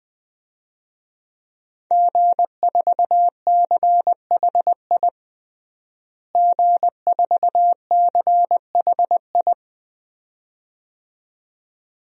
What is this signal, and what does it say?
Morse code, heard as a keyed tone, about 20 wpm, 700 Hz. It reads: G4CHI G4CHI
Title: DXCC-100 Call Signs - 5 Characters - Repeated 2x - 20wpm